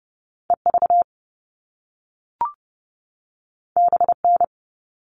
Morse code, heard as a keyed tone, about 30 wpm, 700 Hz. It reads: EV BN
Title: All Cut Numbers - 2-Digit 30wpm